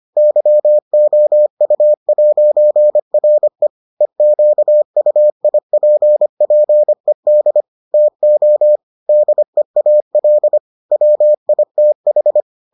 Morse code, heard as a keyed tone, about 25 wpm, 600 Hz. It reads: YOU'RE EQUIPPED TO DEAL WITH